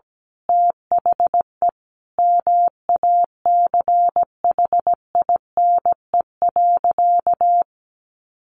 Morse code, heard as a keyed tone, about 17 wpm, 700 Hz. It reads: THE MACHINE.
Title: A Sound Of Thunder 17wpm